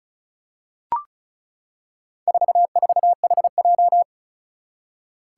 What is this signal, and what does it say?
Morse code, heard as a keyed tone, about 36 wpm, 700 Hz. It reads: 44HJ
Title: Four Challenging Letter-Number Combinations 35wpm